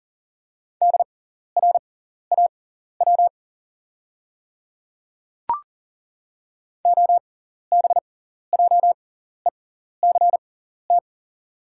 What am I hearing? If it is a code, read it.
Morse code, heard as a keyed tone, about 40 wpm, spaced as for 14 wpm, 690 Hz. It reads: DRAW OBJECT
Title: 6x Character Spacing - Top 500 Words - 40wpm